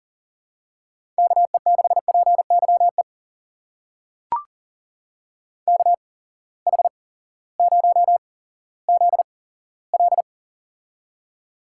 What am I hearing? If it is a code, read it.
Morse code, heard as a keyed tone, about 38 wpm, spaced as for 18 wpm, 700 Hz. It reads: KE6PYE KH0ZL
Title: USA Call Signs - 8x Character Spacing - 40wpm